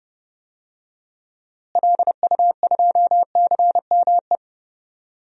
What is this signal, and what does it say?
Morse code, heard as a keyed tone, about 31 wpm, 700 Hz. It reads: LU2CME
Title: DXCC-100 Call Signs - 6 Characters - Repeated 3x - 30wpm